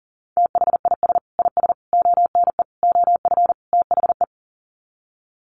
Morse code, heard as a keyed tone, about 40 wpm, 700 Hz. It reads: THIS IS ONE OF THE
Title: Sets of 5 Words 40wpm